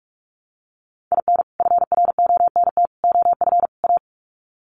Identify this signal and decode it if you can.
Morse code, heard as a keyed tone, about 44 wpm, 700 Hz. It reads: IN FRONT OF A
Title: Sets of 4 Words 45wpm